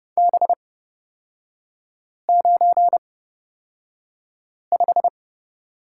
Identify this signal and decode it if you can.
Morse code, heard as a keyed tone, about 29 wpm, spaced as for 5 wpm, 700 Hz. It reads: B95
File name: Lesson 37e - Letter x - Three Characters 30wpm